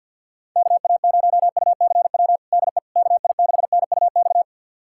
Morse code, heard as a keyed tone, about 46 wpm, 700 Hz. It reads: KA0UCW DE KI6NUX
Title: USA Callsign DE Callsign - 6 Characters - 50wpm